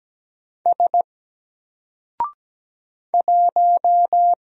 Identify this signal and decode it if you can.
Morse code, heard as a keyed tone, about 17 wpm, 700 Hz. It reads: S 1
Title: Single Character (Letter:Number:Punctuation:Prosign) Alt 17wpm